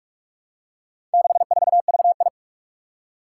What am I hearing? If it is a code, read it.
Morse code, heard as a keyed tone, about 44 wpm, 700 Hz. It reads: 64VI
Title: Four Challenging Letter-Number Combinations 45wpm